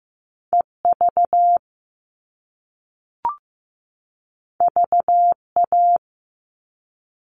Morse code, heard as a keyed tone, about 15 wpm, 700 Hz. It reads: EV VA
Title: All Cut Numbers - 2-Digit 15wpm